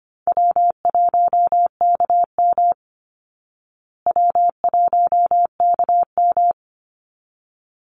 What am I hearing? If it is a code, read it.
Morse code, heard as a keyed tone, about 25 wpm, 700 Hz. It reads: W1KM W1KM
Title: Call Signs Repeated 3x - 25wpm